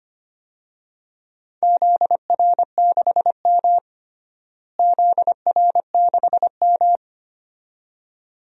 Morse code, heard as a keyed tone, about 25 wpm, 700 Hz. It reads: ZR6M ZR6M
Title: DXCC-100 Call Signs - 4 Characters - Repeated 2x - 25wpm